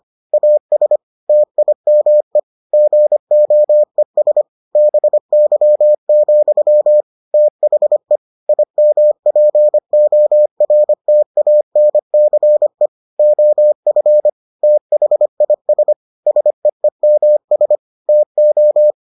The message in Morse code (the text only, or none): AS TIME GOES BY, THE IMPORTANCE OF THIS SEEMS TO